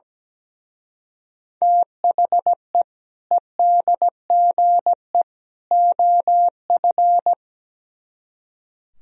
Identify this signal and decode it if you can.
Morse code, heard as a keyed tone, about 19 wpm, 700 Hz. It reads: THE EDGE OF
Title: Speed-Racing - Sets of 3 Words 17wpm